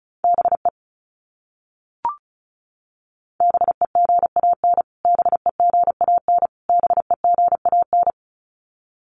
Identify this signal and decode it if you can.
Morse code, heard as a keyed tone, about 34 wpm, 700 Hz. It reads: BE BEGAN BEGAN BEGAN
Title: Top 500 Words Repeated 35wpm